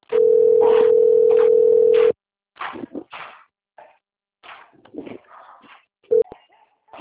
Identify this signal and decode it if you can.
Ringback tone (North America), 440 + 480 Hz.